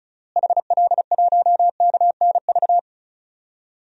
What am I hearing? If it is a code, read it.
Morse code, heard as a keyed tone, about 36 wpm, 700 Hz. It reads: HL1KNV
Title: DXCC-100 Call Signs - 6 Characters - Repeated 3x - 35wpm